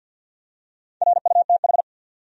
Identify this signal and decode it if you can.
Morse code, heard as a keyed tone, about 50 wpm, 700 Hz. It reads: RUTH
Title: USA Common Names - 50wpm